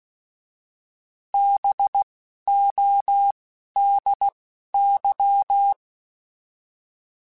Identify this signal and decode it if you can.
Morse code, heard as a keyed tone, about 16 wpm, 800 Hz. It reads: B O D Y